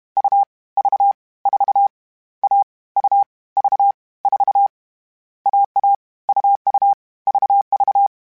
Morse code, heard as a keyed tone, about 32 wpm, 800 Hz. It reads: U V 4 A U V 4 AA UU VV
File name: Difficult Combos / Difficult Combos-mp3s / AUV AUV4 @30